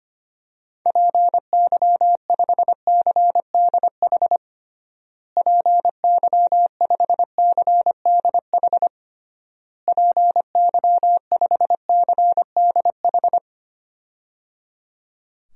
Morse code, heard as a keyed tone, about 25 wpm, 700 Hz. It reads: PY5CDH PY5CDH PY5CDH
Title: DXCC-100 Call Signs - 6 Characters - Repeated 3x - 25wpm